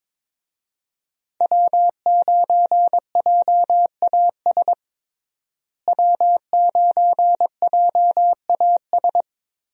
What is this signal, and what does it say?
Morse code, heard as a keyed tone, about 22 wpm, 700 Hz. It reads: W9JAS W9JAS
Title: USA Call Signs - 5 Characters - Repeated 3x - 22wpm